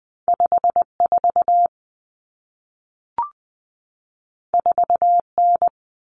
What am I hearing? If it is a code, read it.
Morse code, heard as a keyed tone, about 20 wpm, 700 Hz. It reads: 54 4N
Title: Common Cut Numbers - 2-Digit 20wpm